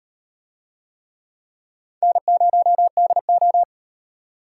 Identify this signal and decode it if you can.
Morse code, heard as a keyed tone, about 38 wpm, 700 Hz. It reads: N0DO